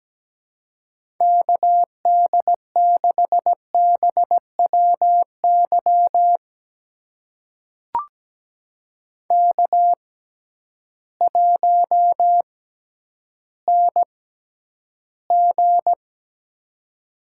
Morse code, heard as a keyed tone, about 17 wpm, spaced as for 10 wpm, 700 Hz. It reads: KD6BWY K1NG
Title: USA Call Signs - 6x Character Spacing - 17wpm